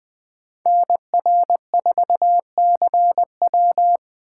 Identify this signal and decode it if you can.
Morse code, heard as a keyed tone, about 20 wpm, 700 Hz. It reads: NR4CW
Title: Speed Racing - USA 2x2 Call Signs 17wpm